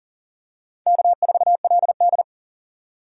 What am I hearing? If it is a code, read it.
Morse code, heard as a keyed tone, about 40 wpm, 700 Hz. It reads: K4LD
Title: USA Call Signs - 4 Characters - 3x Character Spacing - 40wpm